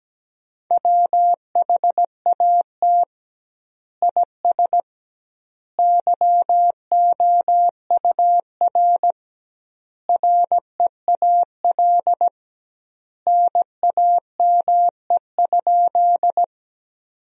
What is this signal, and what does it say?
Morse code, heard as a keyed tone, about 17 wpm, 700 Hz. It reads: WHAT IS YOUR REAL NAME?
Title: 2x Word-Spacing - Sentences from Top 300 Words - 17wpm